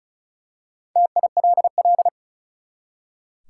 Morse code, heard as a keyed tone, about 36 wpm, 700 Hz. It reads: TILL
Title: Four Letter Words - 35wpm